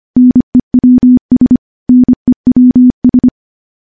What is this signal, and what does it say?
Morse code, heard as a keyed tone, about 25 wpm, 250 Hz. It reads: NEWS NEWS
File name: morse-training-25WPM